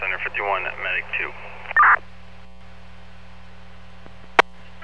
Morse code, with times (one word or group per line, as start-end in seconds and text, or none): none